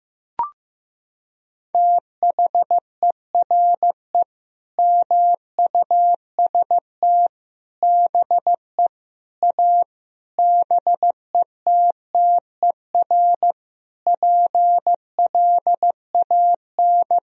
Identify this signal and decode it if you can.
Morse code, heard as a keyed tone, about 14 wpm, 700 Hz. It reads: THERE MUST BE A BETTER PLAN
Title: Sentences from Top 1000 Words - Review Part 4 - 15wpm